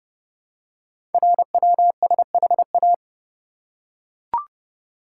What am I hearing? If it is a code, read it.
Morse code, heard as a keyed tone, about 30 wpm, 700 Hz. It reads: RWSHA
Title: Code Groups - 5 Letters - 30wpm